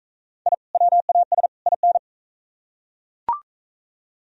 Morse code, heard as a keyed tone, about 39 wpm, 700 Hz. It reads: I WAS IN